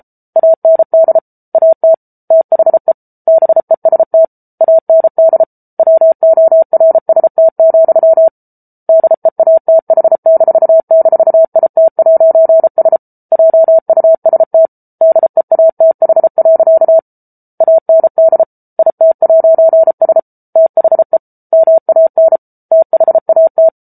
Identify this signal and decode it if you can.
Morse code, heard as a keyed tone, about 34 wpm, 650 Hz. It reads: AND AT THE BEST AND WORST, DEATH--IT'S JUST DEATH. AND IT'S THE MAN THAT